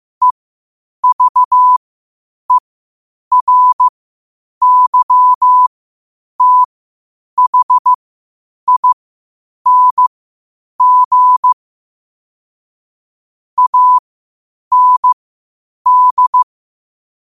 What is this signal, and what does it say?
Morse code, heard as a keyed tone, about 15 wpm, spaced as for 8 wpm, 1000 Hz. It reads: EVERYTHING AND
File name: qotdfiles / 8wpm